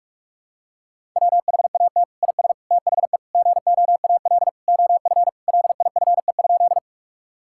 Morse code, heard as a keyed tone, about 46 wpm, 690 Hz. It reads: WHAT IS THE GOAL OF LIFE?
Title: Sentences from Top 1000 Words - Review Part 5 - 45wpm